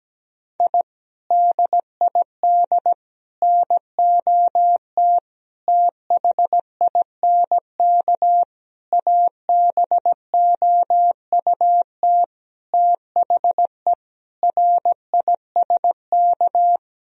Morse code, heard as a keyed tone, about 17 wpm, 700 Hz. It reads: I DID NOT THINK ABOUT THE RISK
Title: Sentences from Top 1000 Words - Review Part 4 - 17wpm